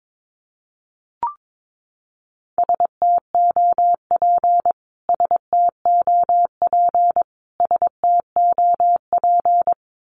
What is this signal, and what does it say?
Morse code, heard as a keyed tone, about 22 wpm, 700 Hz. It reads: STOP STOP STOP
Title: Top 500 Words Repeated 22wpm